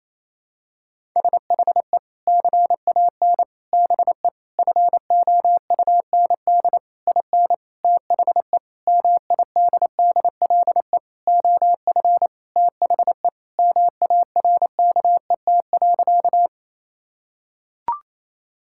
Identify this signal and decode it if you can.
Morse code, heard as a keyed tone, about 28 wpm, 700 Hz. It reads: SHE CAN BE FOUND IN THE MIDDLE OF THE MARKET.
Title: Sentences from Top 1000 Words - Review Part 6 - 28wpm